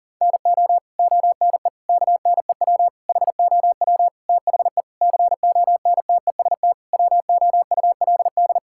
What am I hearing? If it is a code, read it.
Morse code, heard as a keyed tone, about 40 wpm, 700 Hz. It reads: NO ONE KNEW HOW THE CONTEST WOULD
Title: Sentences from Top 1600 Words 40wpm